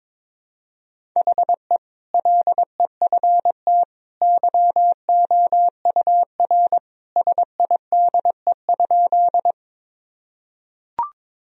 Morse code, heard as a keyed tone, about 21 wpm, 700 Hz. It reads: HE LEFT YOUR SIDE?
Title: Sentences from Top 1000 Words - Review Part 1 - 22wpm